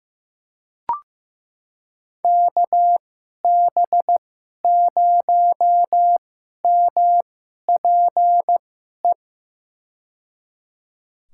Morse code, heard as a keyed tone, about 15 wpm, spaced as for 11 wpm, 700 Hz. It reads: KB0MPE
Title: USA Call Signs - 6 Characters - 2x Character Spacing - 15wpm